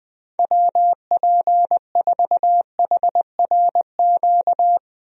Morse code, heard as a keyed tone, about 20 wpm, 700 Hz. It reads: WP4HRQ
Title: USA Call Signs - 6 Characters - 20wpm